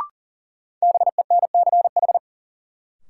Morse code, heard as a keyed tone, about 40 wpm, 700 Hz. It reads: BENCH
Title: Sentences from Top 1400 Words - New Word Drill - 40wpm